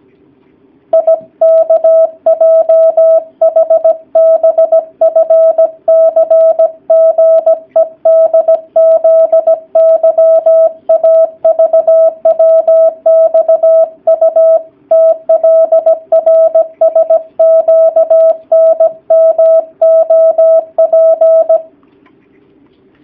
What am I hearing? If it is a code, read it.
Morse code, heard as a keyed tone, about 17 wpm, 650 Hz. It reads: IKJHBFCGEDZYAVWXUTLRSQNMOP
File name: jh8cbh / 2019cw / sp16- w